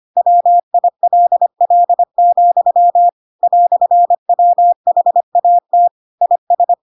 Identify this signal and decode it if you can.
Morse code, heard as a keyed tone, about 25 wpm, 700 Hz. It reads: WILL, "WHAT IS
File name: Pinocchio / Pinocchio ch2 25 25 ik1hge